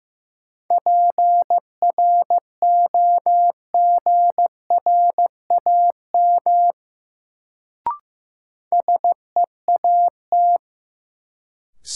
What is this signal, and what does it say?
Morse code, heard as a keyed tone, about 15 wpm, 700 Hz. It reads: PROGRAM SEAT
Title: Sentences from Top 1500 Words - New Word Drill Alt - 15wpm